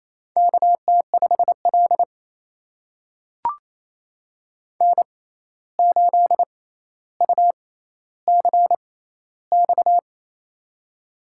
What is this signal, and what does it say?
Morse code, heard as a keyed tone, about 29 wpm, spaced as for 14 wpm, 700 Hz. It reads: KT5L N8UCX
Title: USA Call Signs - 6x Character Spacing - 28wpm